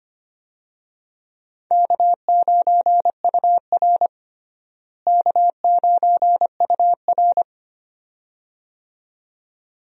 Morse code, heard as a keyed tone, about 25 wpm, 700 Hz. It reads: K9UR K9UR
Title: USA Call Signs - 4 Characters - Repeated 2x - 25wpm